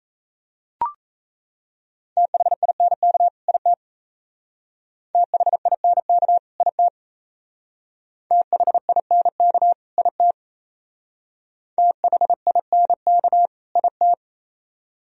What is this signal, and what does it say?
Morse code, heard as a keyed tone, about 35 wpm, 700 Hz. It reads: THINK IT THINK IT THINK IT THINK IT
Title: Speed-Racing - Sets of 2 Words 28wpm